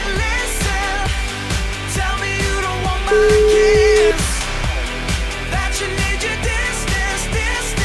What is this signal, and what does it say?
Ringback tone (Europe, 425 Hz), with noise.